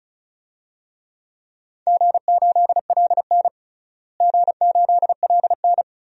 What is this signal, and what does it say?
Morse code, heard as a keyed tone, about 34 wpm, 700 Hz. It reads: G8LN G8LN
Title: DXCC-100 Call Signs - 4 Characters - Repeated 2x - 35wpm